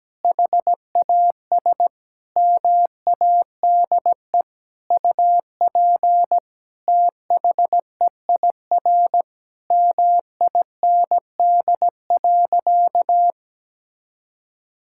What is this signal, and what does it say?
Morse code, heard as a keyed tone, about 17 wpm, 700 Hz. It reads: HAS MADE UP THEIR MIND.
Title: Sentences from Top 1000 Words - Review Part 5 - 17wpm